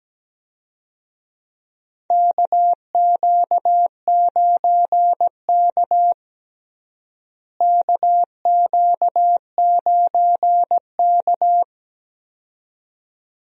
Morse code, heard as a keyed tone, about 17 wpm, 700 Hz. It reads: KQ9K KQ9K
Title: USA Call Signs - 4 Characters - Repeated 2x - 17wpm